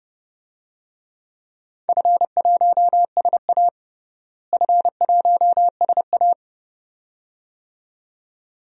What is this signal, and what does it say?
Morse code, heard as a keyed tone, about 30 wpm, 700 Hz. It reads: F1SA F1SA
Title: DXCC-100 Call Signs - 4 Characters - Repeated 2x - 30wpm